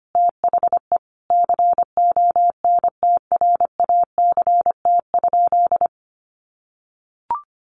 Morse code, heard as a keyed tone, about 25 wpm, 700 Hz. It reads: THE CONTRACT?
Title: Sentences from Top 1400 Words 25wpm